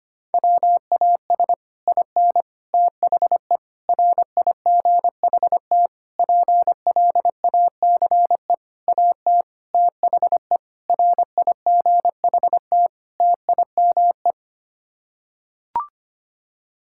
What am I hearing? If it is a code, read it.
Morse code, heard as a keyed tone, about 25 wpm, 700 Hz. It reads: WAS IN THE RIGHT PLACE AT THE RIGHT TIME
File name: Sets of 9 Words - 25wpm